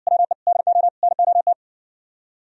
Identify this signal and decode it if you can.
Morse code, heard as a keyed tone, about 67 wpm, 700 Hz. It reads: WE DO NOT